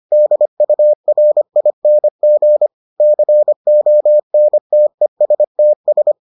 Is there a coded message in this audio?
Yes: Morse code from 0.1 to 6.1 s, heard as a keyed tone, about 25 wpm, 600 Hz. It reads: DURING CONTESTS